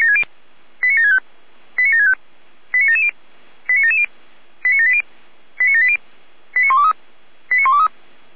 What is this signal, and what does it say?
Five-tone selective call (ZVEI), heard as quick runs of five tones, tones of about 70 ms. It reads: 89765 89765 89770 89770 89780 89780 89123 89123